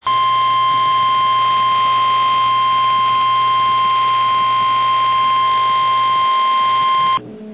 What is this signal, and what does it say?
Emergency Alert System (SAME) tones (the 1050 Hz attention tone), with noise. It is heard from 0.0 to 7.2 s.
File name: eas / KNCYKBIE / KNCYKBIE